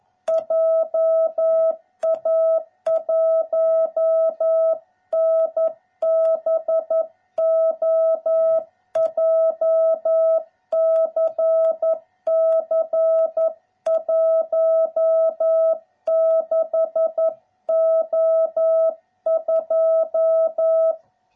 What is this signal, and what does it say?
Morse code, heard as a keyed tone, about 11 wpm, 660 Hz. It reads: JA1NBO JCC 16O2